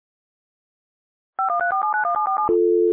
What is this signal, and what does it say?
DTMF, heard as telephone touch tones, tones of about 120 ms. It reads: 5134791747
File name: radiocap / REC z